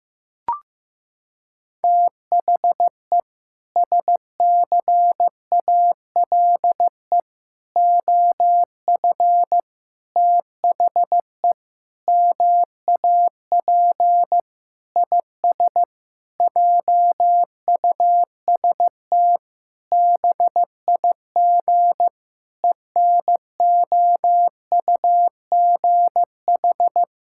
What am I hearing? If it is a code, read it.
Morse code, heard as a keyed tone, about 15 wpm, 700 Hz. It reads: THE SCALE OF THE MAP IS JUST BIG ENOUGH
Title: Sentences from Top 1000 Words - Review Part 5 - 15wpm